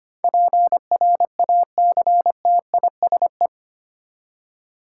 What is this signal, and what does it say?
Morse code, heard as a keyed tone, about 25 wpm, 700 Hz. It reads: PRACTISE
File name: Eight Letter Words - 25wpm